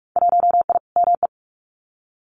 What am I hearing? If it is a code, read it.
Morse code, heard as a keyed tone, about 46 wpm, 700 Hz. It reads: 1I ME
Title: Winter Field Day Exchanges 45wpm